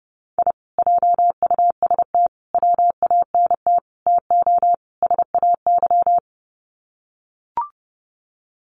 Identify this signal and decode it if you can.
Morse code, heard as a keyed tone, about 30 wpm, 700 Hz. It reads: I JUST WANT TO SAY
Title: Sets of 5 Words Encore 30wpm